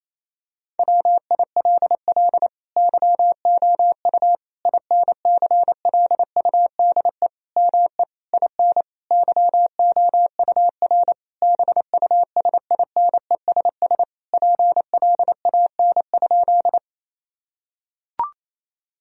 Morse code, heard as a keyed tone, about 28 wpm, 700 Hz. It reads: WILL YOU INCLUDE ME IN YOUR BUSINESS PLAN?
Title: Sentences from Top 1000 Words - Review Part 3 - 28wpm